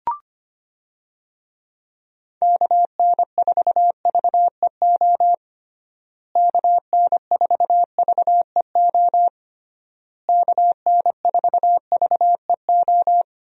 Morse code, heard as a keyed tone, about 25 wpm, 700 Hz. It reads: KN4VEO KN4VEO KN4VEO
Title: USA Call Signs - 6 Characters - Repeated 3x - 25wpm